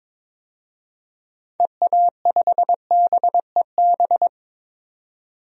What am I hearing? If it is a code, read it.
Morse code, heard as a keyed tone, about 22 wpm, 700 Hz. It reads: EA5BEB
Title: DXCC-100 Call Signs - 6 Characters - 22wpm